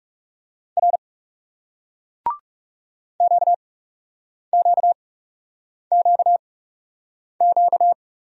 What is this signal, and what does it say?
Morse code, heard as a keyed tone, about 38 wpm, 700 Hz. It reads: R Q Q Q Q